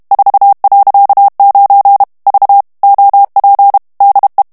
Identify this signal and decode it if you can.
Morse code, heard as a keyed tone, about 32 wpm, 800 Hz. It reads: 4.9 V OP DE
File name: HOLLANDAIS texte clair 2 vitesse 30 mots minute